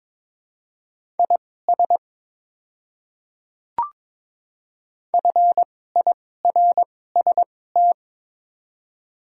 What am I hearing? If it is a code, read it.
Morse code, heard as a keyed tone, about 22 wpm, spaced as for 16 wpm, 700 Hz. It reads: IS FIRST
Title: 2x Character Spacing - Top 100 Words - 22wpm